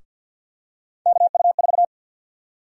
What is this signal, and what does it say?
Morse code, heard as a keyed tone, about 48 wpm, 700 Hz. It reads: KU4
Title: USA Call Sign Prefixes - Repeated 2x - 50wpm